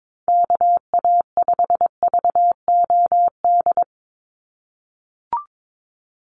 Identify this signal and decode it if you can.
Morse code, heard as a keyed tone, about 22 wpm, 700 Hz. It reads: KA5VOD
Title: USA Call Signs 22wpm